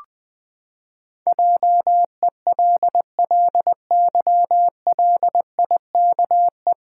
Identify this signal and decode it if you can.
Morse code, heard as a keyed tone, about 20 wpm, 700 Hz. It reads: JELLYLIKE
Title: Pangram Challenge v2 - 20wpm